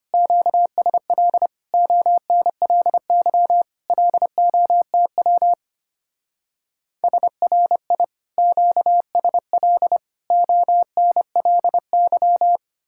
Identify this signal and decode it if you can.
Morse code, heard as a keyed tone, about 30 wpm, 700 Hz. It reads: QSL ONLY LOTW SRI QSL ONLY
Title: Speed-Racing Common QSO Phrases - 25wpm